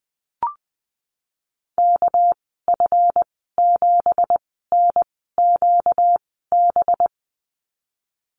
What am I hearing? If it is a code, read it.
Morse code, heard as a keyed tone, about 20 wpm, spaced as for 14 wpm, 700 Hz. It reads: KF7NQB
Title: 245 Lesson 39g - Letter z - Call Signs 20wpm